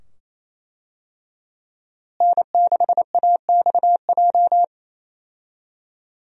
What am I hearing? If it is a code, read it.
Morse code, heard as a keyed tone, about 27 wpm, 700 Hz. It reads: N6AXJ
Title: USA Call Signs - 5 Characters - Repeated 2x - 28wpm